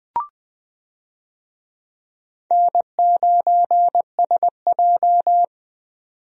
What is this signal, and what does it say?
Morse code, heard as a keyed tone, about 20 wpm, 700 Hz. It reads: N9SJ
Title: USA Call Signs - 4 Characters - 20wpm